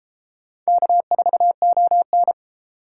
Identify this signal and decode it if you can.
Morse code, heard as a keyed tone, about 32 wpm, 700 Hz. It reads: K4ON